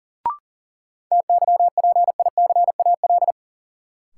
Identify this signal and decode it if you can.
Morse code, heard as a keyed tone, about 40 wpm, 700 Hz. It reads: TYPICAL
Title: Sentences from Top 1100 Words - New Word Drill Alt - 40wpm